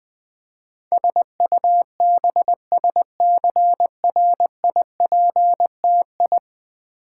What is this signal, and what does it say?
Morse code, heard as a keyed tone, about 20 wpm, 700 Hz. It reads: SUBSCRIPTI
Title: Build A Word - 5 Letters At A Time - 12-Letter Words - 20wpm